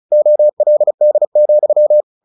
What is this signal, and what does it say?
Morse code, heard as a keyed tone, about 34 wpm, 600 Hz. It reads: OLD,
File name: the old man and the sea / the old man and the sea 1 35wpm